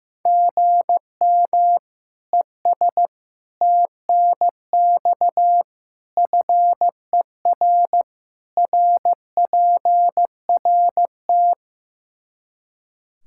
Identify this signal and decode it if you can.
Morse code, heard as a keyed tone, about 16 wpm, 700 Hz. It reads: GM ES TNX FER RPRT
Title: K1USN SST QSO Practice 15wpm